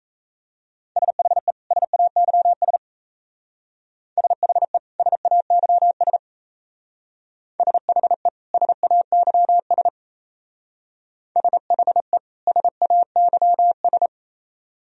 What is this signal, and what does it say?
Morse code, heard as a keyed tone, about 33 wpm, 700 Hz. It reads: SHE SAYS SHE SAYS SHE SAYS SHE SAYS
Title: Speed-Racing - Sets of 2 Words 28wpm